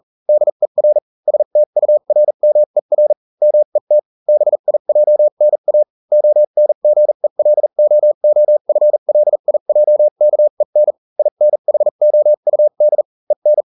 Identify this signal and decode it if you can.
Morse code, heard as a keyed tone, about 40 wpm, spaced as for 35 wpm, 600 Hz. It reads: DER STURMER MET BIJNA ONGELOOFLIJKEN INHOUD EN